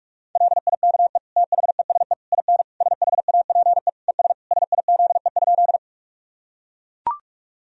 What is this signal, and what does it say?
Morse code, heard as a keyed tone, about 41 wpm, 700 Hz. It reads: LIKE THESE IN SHAPE ES SIZE?
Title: Sentences from Top 500 Words and Abbreviations 45wpm